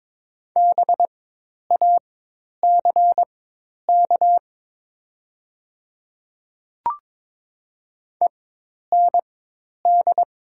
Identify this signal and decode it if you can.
Morse code, heard as a keyed tone, about 22 wpm, spaced as for 10 wpm, 700 Hz. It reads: BACK END
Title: 4x Character Spacing - Top 200 Words - 22wpm